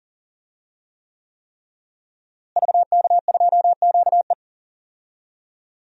Morse code, heard as a keyed tone, about 39 wpm, 700 Hz. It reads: VK2QE
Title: DXCC-100 Call Signs - 5 Characters - Repeated 3x - 40wpm